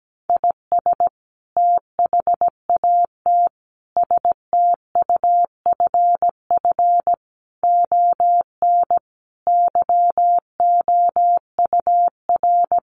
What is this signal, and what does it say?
Morse code, heard as a keyed tone, about 17 wpm, 700 Hz. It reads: IS THAT STUFF ON YOUR
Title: Sentences from Top 1400 Words 17wpm